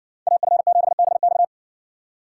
Morse code, heard as a keyed tone, about 67 wpm, 700 Hz. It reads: AF7BX